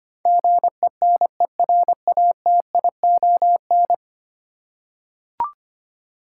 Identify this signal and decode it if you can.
Morse code, heard as a keyed tone, about 25 wpm, 700 Hz. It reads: GENERATION
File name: 10 Letter Words - First 3 Letters Given - 25wpm